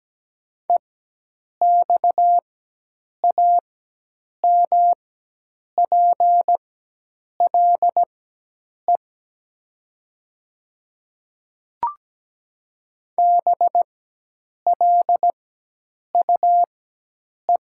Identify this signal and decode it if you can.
Morse code, heard as a keyed tone, about 17 wpm, spaced as for 8 wpm, 700 Hz. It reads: EXAMPLE BLUE
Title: 4x Character Spacing - Top 500 Words - 17wpm